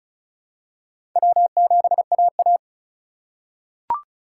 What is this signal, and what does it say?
Morse code, heard as a keyed tone, about 34 wpm, 700 Hz. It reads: W7AA
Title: USA Call Signs - 4 Characters - 3x Character Spacing - 35wpm